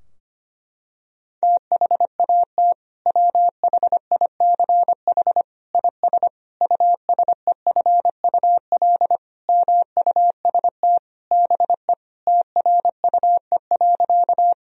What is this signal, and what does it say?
Morse code, heard as a keyed tone, about 25 wpm, 700 Hz. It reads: THAT WHICH IS USEFUL MUST BE TRUE.
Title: Sentences from Top 1000 Words - Review Part 4 - 25wpm